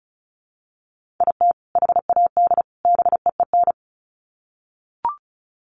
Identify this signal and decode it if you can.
Morse code, heard as a keyed tone, about 34 wpm, 700 Hz. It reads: IT HAD BEEN
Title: Sets of 3 Words 35wpm